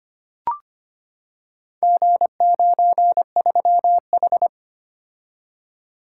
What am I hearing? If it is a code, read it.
Morse code, heard as a keyed tone, about 25 wpm, 700 Hz. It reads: G93H
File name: practice / 25wpm / Four Letter-Number Combinations - 25wpm